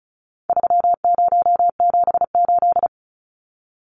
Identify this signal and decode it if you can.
Morse code, heard as a keyed tone, about 34 wpm, 700 Hz. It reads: 3078
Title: Numbers - 4-Digit 35wpm